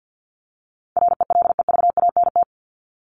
Morse code, heard as a keyed tone, about 50 wpm, 700 Hz. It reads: RELEVANT
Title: Top 1001-1250 Words 50wpm